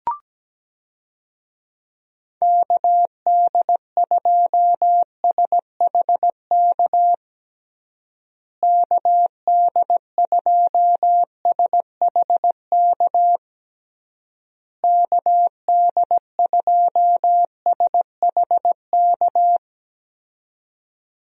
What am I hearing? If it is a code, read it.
Morse code, heard as a keyed tone, about 17 wpm, 700 Hz. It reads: KD2SHK KD2SHK KD2SHK
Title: USA Call Signs - 6 Characters - Repeated 3x - 17wpm